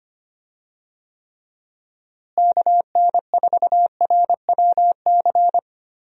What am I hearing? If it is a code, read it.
Morse code, heard as a keyed tone, about 25 wpm, 700 Hz. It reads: KN4RWC
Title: USA Call Signs - 6 Characters - Repeated 2x - 25wpm